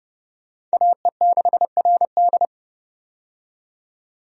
Morse code, heard as a keyed tone, about 30 wpm, 700 Hz. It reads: AE6RD
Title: USA Call Signs - 5 Characters - 30wpm